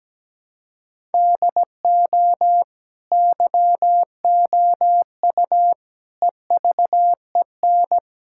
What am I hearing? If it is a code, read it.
Morse code, heard as a keyed tone, about 17 wpm, 700 Hz. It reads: DO YOU EVEN
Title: Sentences from Top 1000 Words - Review Part 5 - 17wpm